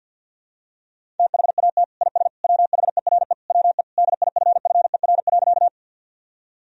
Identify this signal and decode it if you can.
Morse code, heard as a keyed tone, about 52 wpm, 700 Hz. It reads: THAT IS WHERE WE DIFFER.
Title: Sentences from Top 1000 Words - Review Part 1 - 50wpm